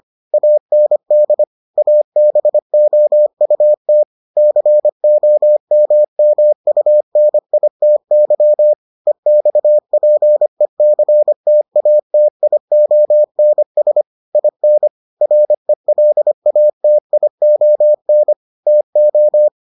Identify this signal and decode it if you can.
Morse code, heard as a keyed tone, about 25 wpm, 600 Hz. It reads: AND ABOUT COMMUNITY EXPECTATIONS IN RELATION TO